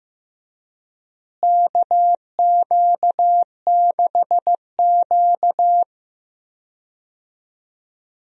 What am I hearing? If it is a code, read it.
Morse code, heard as a keyed tone, about 15 wpm, 700 Hz. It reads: KQ6Q
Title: USA Call Signs - 4 Characters - Repeated 3x - 15wpm